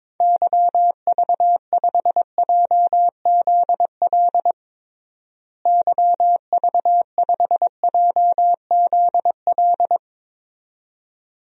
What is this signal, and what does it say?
Morse code, heard as a keyed tone, about 22 wpm, 700 Hz. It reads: YV5JZL YV5JZL
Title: DXCC-100 Call Signs - 6 Characters - Repeated 2x - 22wpm